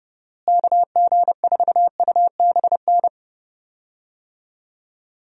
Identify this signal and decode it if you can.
Morse code, heard as a keyed tone, about 31 wpm, 700 Hz. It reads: KG4UBN
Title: USA Call Signs - 6 Characters - Repeated 3x - 30wpm